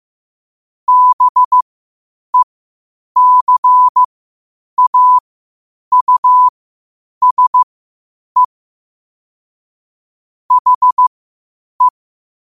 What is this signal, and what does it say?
Morse code, heard as a keyed tone, about 15 wpm, spaced as for 8 wpm, 1000 Hz. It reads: BECAUSE HE
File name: qotdfiles / 8wpm